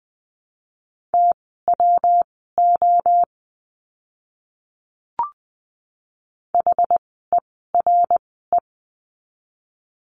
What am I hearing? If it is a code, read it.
Morse code, heard as a keyed tone, about 20 wpm, spaced as for 14 wpm, 700 Hz. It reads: TWO HERE
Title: 244 Lesson 39f - Letter z - Words 20wpm